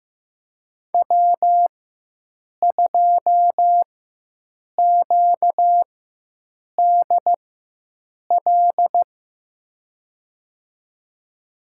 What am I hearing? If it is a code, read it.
Morse code, heard as a keyed tone, about 15 wpm, spaced as for 7 wpm, 700 Hz. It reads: W2QDL
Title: USA Call Signs - 5 Characters - 4x Character Spacing - 15wpm